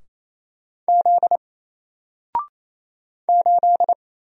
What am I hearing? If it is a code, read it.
Morse code, heard as a keyed tone, about 27 wpm, 700 Hz. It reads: Z 8
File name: Single-Letter-Number-28wpm